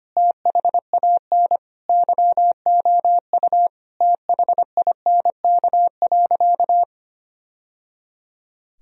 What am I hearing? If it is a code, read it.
Morse code, heard as a keyed tone, about 25 wpm, 700 Hz. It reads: THAN YOU THINK.
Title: Sentences from Top 1000 Words - Review Part 3 - 25wpm